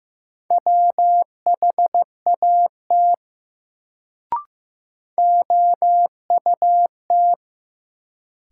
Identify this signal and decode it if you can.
Morse code, heard as a keyed tone, about 15 wpm, 700 Hz. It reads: WHAT OUT
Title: Sets of 2 Words - New Word Drill - 15wpm